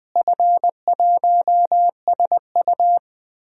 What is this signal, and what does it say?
Morse code, heard as a keyed tone, about 20 wpm, 700 Hz. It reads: F1SU
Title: DXCC-100 Call Signs - 4 Characters - Repeated 2x - 20wpm